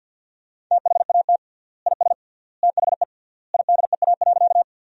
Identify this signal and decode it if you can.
Morse code, heard as a keyed tone, about 47 wpm, 700 Hz. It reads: THAT IS THE IDEA.
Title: 3x Word-Spacing - Sentences from Top 300 Words - 50wpm